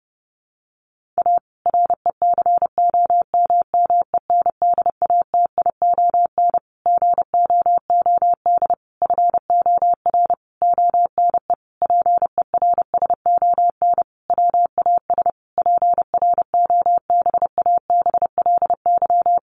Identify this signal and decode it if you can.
Morse code, heard as a keyed tone, about 30 wpm, 700 Hz. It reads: A RECOMMENDATION GOOD FOR ONE PERSON WAS PROBABLY
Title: Sentences from Top 1400 Words 30wpm